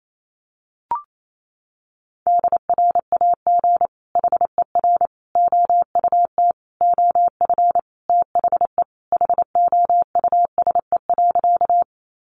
Morse code, heard as a keyed tone, about 28 wpm, 700 Hz. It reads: DRAG HER OUT OF THE HOUSE.
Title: Sentences from Top 1500 Words 28wpm